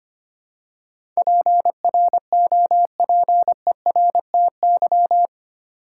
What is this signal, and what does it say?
Morse code, heard as a keyed tone, about 25 wpm, 700 Hz. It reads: PROPERTY
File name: 8 Letter Words - First 3 Letters Given - 25wpm